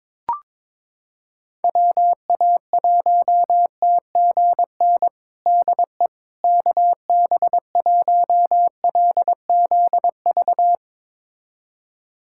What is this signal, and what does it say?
Morse code, heard as a keyed tone, about 22 wpm, 700 Hz. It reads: WA1TGN DE KB1LZV
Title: USA Callsign DE Callsign - 6 Characters - 22wpm